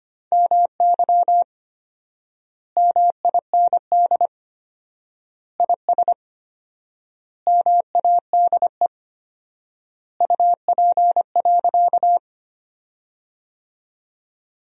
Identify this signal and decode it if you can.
Morse code, heard as a keyed tone, about 25 wpm, 700 Hz. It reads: MY MIND IS MADE UP.
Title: 4x Word-Spacing - Sentences from Top 500 Words - 25wpm